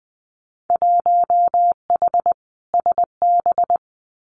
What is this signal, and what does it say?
Morse code, heard as a keyed tone, about 20 wpm, 700 Hz. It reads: 1H SB
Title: Winter Field Day Exchanges 20wpm